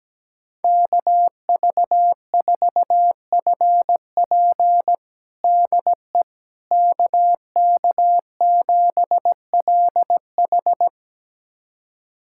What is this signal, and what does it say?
Morse code, heard as a keyed tone, about 17 wpm, 700 Hz. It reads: KV4FP DE KK7LH
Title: USA Callsign DE Callsign - 5 Characters - 17wpm